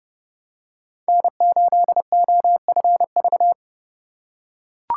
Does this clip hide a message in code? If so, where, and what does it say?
Yes: Morse code from 1.1 to 3.5 s, heard as a keyed tone, about 31 wpm, 700 Hz. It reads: N8OFV